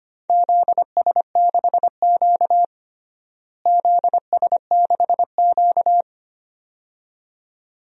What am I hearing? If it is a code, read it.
Morse code, heard as a keyed tone, about 25 wpm, 700 Hz. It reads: ZS6Q ZS6Q
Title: DXCC-100 Call Signs - 4 Characters - Repeated 2x - 25wpm